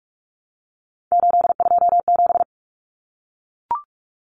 Morse code, heard as a keyed tone, about 44 wpm, 700 Hz. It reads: 827
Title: Common Cut Numbers - 3-Digit 45wpm